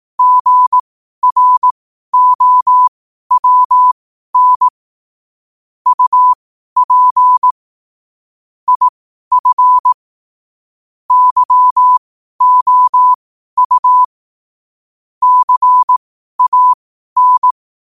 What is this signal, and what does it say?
Morse code, heard as a keyed tone, about 18 wpm, spaced as for 12 wpm, 1000 Hz. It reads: GROWN UP IF YOU CAN
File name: qotdfiles / 12wpm